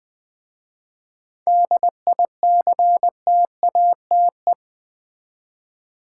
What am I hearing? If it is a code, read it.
Morse code, heard as a keyed tone, about 20 wpm, 700 Hz. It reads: DICTATE
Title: 7 Letter Words - First 3 Letters Given - 20wpm